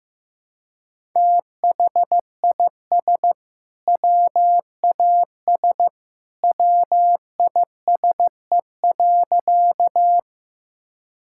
Morse code, heard as a keyed tone, about 15 wpm, 700 Hz. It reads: THIS WAS WISE.
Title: Sentences from Top 1600 Words 15wpm